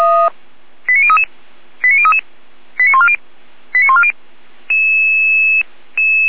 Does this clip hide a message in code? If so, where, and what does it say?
Yes: five-tone selective call (ZVEI) from 0.9 to 4.1 s, heard as quick runs of five tones, tones of about 70 ms. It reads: 89033 89033 89140 89140